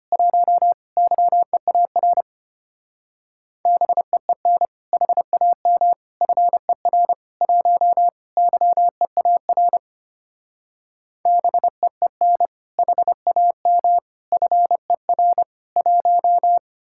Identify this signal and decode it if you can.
Morse code, heard as a keyed tone, about 31 wpm, 700 Hz. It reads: 1 YEAR BEEN HAM FER 1 YEAR BEEN HAM FER 1
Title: Speed-Racing Common QSO Phrases - 25wpm